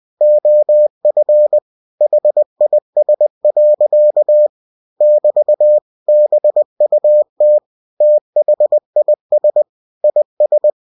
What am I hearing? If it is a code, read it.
Morse code, heard as a keyed tone, about 20 wpm, 600 Hz. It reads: OF HIS. <BT> BUT THIS IS
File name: cw-books / winnie-the-pooh / winnie-20wpm-page-0014